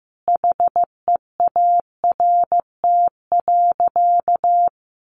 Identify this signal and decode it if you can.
Morse code, heard as a keyed tone, about 14 wpm, 700 Hz. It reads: HEART.
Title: Sentences from Top 1500 Words 15wpm